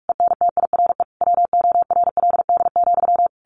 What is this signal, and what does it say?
Morse code, heard as a keyed tone, about 46 wpm, 700 Hz. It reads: ENTIRE WORLD,
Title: A Sound Of Thunder 45wpm